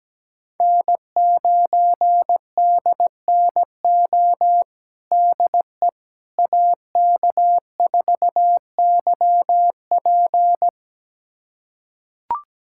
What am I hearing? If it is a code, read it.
Morse code, heard as a keyed tone, about 17 wpm, 700 Hz. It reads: N9DNO DE AK4YP
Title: USA Callsign DE Callsign - 5 Characters - 17wpm